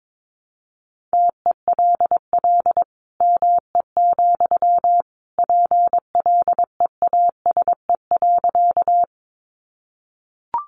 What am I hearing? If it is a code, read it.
Morse code, heard as a keyed tone, about 22 wpm, 700 Hz. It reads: TELL ME, PLEASE.
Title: Sentences from Top 1500 Words 22wpm